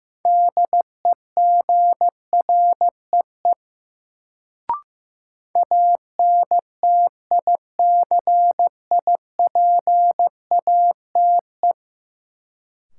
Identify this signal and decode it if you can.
Morse code, heard as a keyed tone, about 15 wpm, 700 Hz. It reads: DEGREE ANTICIPATE
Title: Sentences from Top 1500 Words - New Word Drill - 15wpm